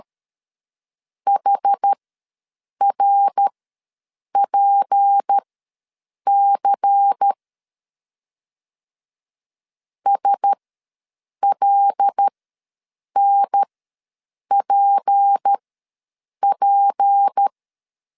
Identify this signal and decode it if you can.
Morse code, heard as a keyed tone, about 13 wpm, spaced as for 7 wpm, 800 Hz. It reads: HRPC SLNPP
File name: LEZIONI / 6 PKX0 / 6 P ws 12wpm 3117